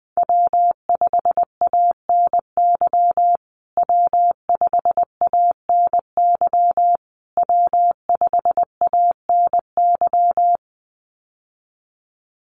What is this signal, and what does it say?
Morse code, heard as a keyed tone, about 20 wpm, 700 Hz. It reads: W5ANY W5ANY W5ANY
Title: USA Call Signs Repeated 3x 20wpm